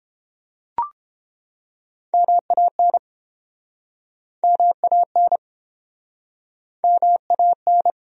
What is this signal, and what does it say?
Morse code, heard as a keyed tone, about 30 wpm, 700 Hz. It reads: MAN MAN MAN